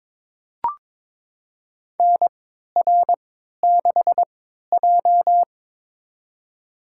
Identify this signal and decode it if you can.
Morse code, heard as a keyed tone, about 22 wpm, spaced as for 13 wpm, 700 Hz. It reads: NR6J
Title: USA Call Signs - 4 Characters - 3x Character Spacing - 22wpm